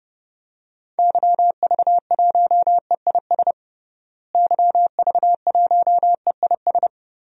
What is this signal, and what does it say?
Morse code, heard as a keyed tone, about 30 wpm, 700 Hz. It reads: YV1EIS YV1EIS
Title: DXCC-100 Call Signs - 6 Characters - Repeated 3x - 30wpm